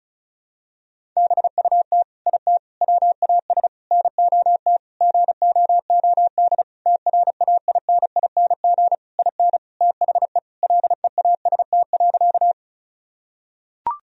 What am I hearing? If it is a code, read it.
Morse code, heard as a keyed tone, about 36 wpm, 700 Hz. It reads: BUT IT WAS NOT GOOD TRAINING IN THE LEAST.
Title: Sentences from Top 1000 Words - Review Part 4 - 35wpm